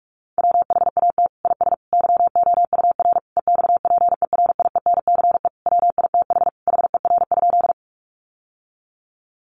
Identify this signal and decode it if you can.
Morse code, heard as a keyed tone, about 44 wpm, 700 Hz. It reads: WHAT IS YOUR EXPERIENCE WITH HER?
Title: Sentences from Top 600 Words 45wpm